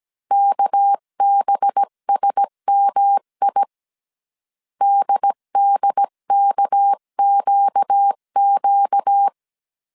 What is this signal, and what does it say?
Morse code, heard as a keyed tone, about 17 wpm, 800 Hz. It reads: KBSMI DDKQQ
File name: CW / DET / 1 6 1500C S C 16wpm 2733